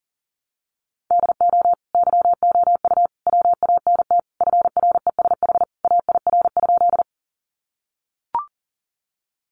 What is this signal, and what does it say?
Morse code, heard as a keyed tone, about 40 wpm, 700 Hz. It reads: DO YOU WANT FRESH AIR?
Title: Sentences from Top 1200 Words 40wpm